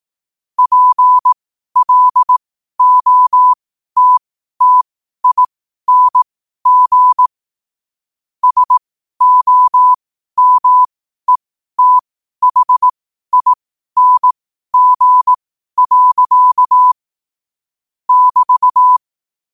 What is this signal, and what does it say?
Morse code, heard as a keyed tone, about 18 wpm, spaced as for 12 wpm, 1000 Hz. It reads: PLOTTING SOMETHING. <BT>
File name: qotdfiles / 12wpm